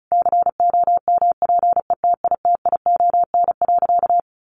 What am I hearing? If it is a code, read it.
Morse code, heard as a keyed tone, about 34 wpm, 700 Hz. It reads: COMPETITION.
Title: Additional Easy Phrases 35wpm